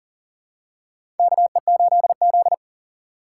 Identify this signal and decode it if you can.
Morse code, heard as a keyed tone, about 40 wpm, 700 Hz. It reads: KE8Z
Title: USA Call Signs - 2x Character Spacing - 40wpm